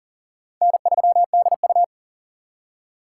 Morse code, heard as a keyed tone, about 40 wpm, 700 Hz. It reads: N3DU
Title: USA Call Signs - 4 Characters - 2x Character Spacing - 40wpm